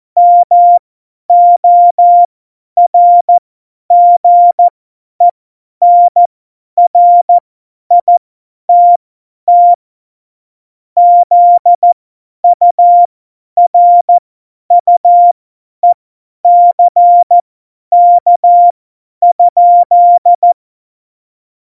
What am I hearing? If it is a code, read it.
Morse code, heard as a keyed tone, about 14 wpm, spaced as for 10 wpm, 700 Hz. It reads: MORGENRITT ZURUECK?
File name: Morsekurs / Witze / Witz 26 Mai 2020, 50 F70